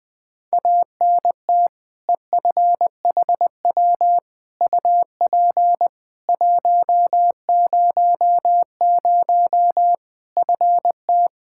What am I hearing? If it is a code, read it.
Morse code, heard as a keyed tone, about 20 wpm, 700 Hz. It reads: ANT EFHW UP 100 FT
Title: Practice QSO - Exchange 2 - Info - 20wpm